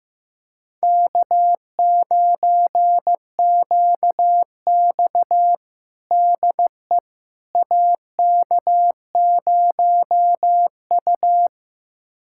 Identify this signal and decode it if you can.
Morse code, heard as a keyed tone, about 15 wpm, 700 Hz. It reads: K9QX DE AK0U
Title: USA Callsign DE Callsign - 4 Characters - 15wpm